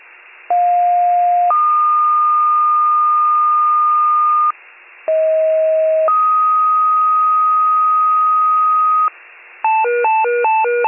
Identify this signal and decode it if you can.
Two-tone page, heard as a one-second tone, then a three-second tone, about 710 then 1220 Hz, about 640 then 1250 Hz. It is heard from 0.5 to 9.1 s.